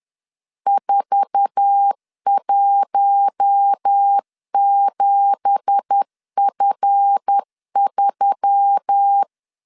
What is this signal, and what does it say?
Morse code, heard as a keyed tone, about 10 wpm, 800 Hz. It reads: 417F3
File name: LEZIONI / 10 SIGNS / 10A SLASH 10wpm 2808